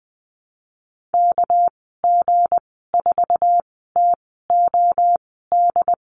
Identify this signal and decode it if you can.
Morse code, heard as a keyed tone, about 20 wpm, spaced as for 15 wpm, 700 Hz. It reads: KG4TOD
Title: 207 Lesson 33g - Letter j - Call Signs 20wpm